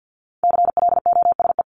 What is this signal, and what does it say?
Morse code, heard as a keyed tone, about 50 wpm, 700 Hz. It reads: CLOSE
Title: ED Suffix - Top 100 Words with Base Words 50wpm